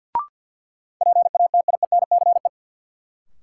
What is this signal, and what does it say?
Morse code, heard as a keyed tone, about 52 wpm, 700 Hz. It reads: PATIENCE
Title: Sentences from Top 1300 Words - New Word Drill - 50wpm